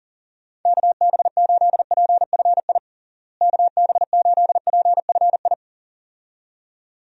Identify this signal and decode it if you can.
Morse code, heard as a keyed tone, about 40 wpm, 700 Hz. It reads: KB8PFI KB8PFI
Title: USA Call Signs - 6 Characters - Repeated 2x - 40wpm